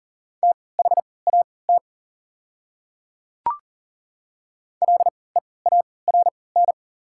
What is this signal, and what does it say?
Morse code, heard as a keyed tone, about 40 wpm, spaced as for 23 wpm, 690 Hz. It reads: THAT LEARN
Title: 3x Character Spacing - Top 300 Words - 40wpm